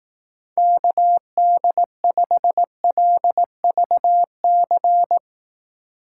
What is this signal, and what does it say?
Morse code, heard as a keyed tone, about 18 wpm, 700 Hz. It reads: KD5LVC
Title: Speed Racing - USA 2x3 Call Signs 15wpm